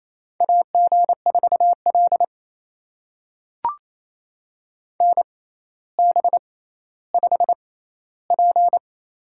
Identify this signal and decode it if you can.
Morse code, heard as a keyed tone, about 28 wpm, spaced as for 14 wpm, 700 Hz. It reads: AG4L NB5P
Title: USA Call Signs - 4 Characters - 6x Character Spacing - 28wpm